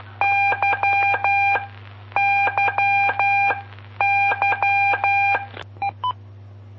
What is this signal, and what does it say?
Morse code, heard as a keyed tone, about 12 wpm, spaced as for 9 wpm, 800 Hz. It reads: YYYE
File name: pp5bar / aulas de telegrafia / Aula-15